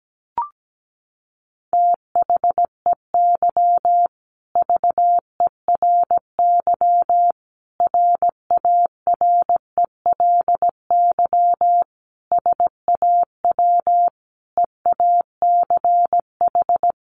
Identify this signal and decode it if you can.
Morse code, heard as a keyed tone, about 17 wpm, 700 Hz. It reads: THEY VERY RARELY SAW EACH
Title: Sentences from Top 1400 Words 17wpm